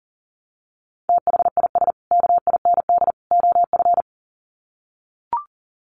Morse code, heard as a keyed tone, about 40 wpm, 700 Hz. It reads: THIS KIND OF
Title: Sets of 3 Words 40wpm